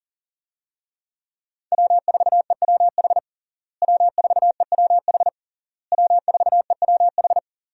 Morse code, heard as a keyed tone, about 40 wpm, 700 Hz. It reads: W4EWH W4EWH W4EWH
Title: USA Call Signs - 5 Characters - Repeated 3x - 40wpm